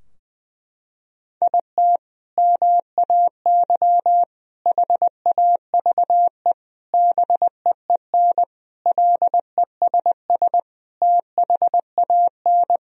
Morse code, heard as a keyed tone, about 21 wpm, 700 Hz. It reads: IT MAY HAVE BEEN LESS THAN
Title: Sentences from Top 1000 Words - Review Part 2 - 20wpm